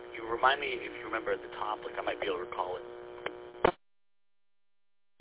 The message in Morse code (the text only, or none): none